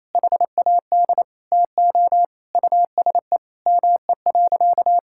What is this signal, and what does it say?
Morse code, heard as a keyed tone, about 27 wpm, 700 Hz. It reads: HAD TO USE ME.
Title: Sentences from Top 1000 Words - Review Part 6 - 28wpm